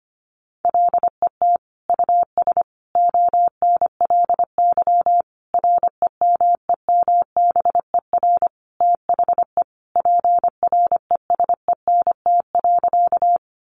Morse code, heard as a keyed tone, about 25 wpm, 700 Hz. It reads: LET US ONLY REMEMBER THE PRESENT.
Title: Sentences from Top 800 Words 25wpm